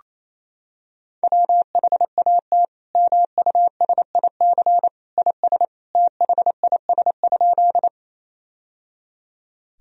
Morse code, heard as a keyed tone, about 28 wpm, 700 Hz. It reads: WHAT MUSIC IS THIS?
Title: Sentences from Top 1000 Words - Review Part 1 - 28wpm